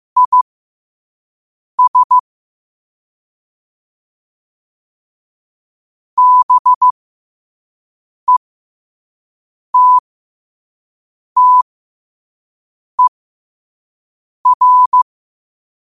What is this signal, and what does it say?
Morse code, heard as a keyed tone, about 15 wpm, spaced as for 5 wpm, 1000 Hz. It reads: IS BETTER